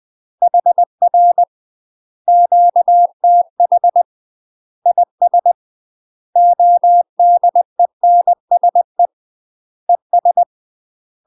Morse code, heard as a keyed tone, about 20 wpm, 700 Hz. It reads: HR QTH IS ODENSE ES
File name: UK MASTER QSOs TRAINING / 20WPM / UK MASTER QSOs 0047